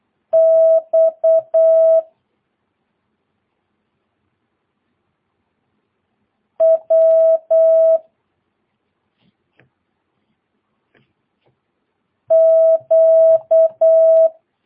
Morse code, heard as a keyed tone, about 8 wpm, 650 Hz. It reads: X W Q
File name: jh8cbh / 2022cw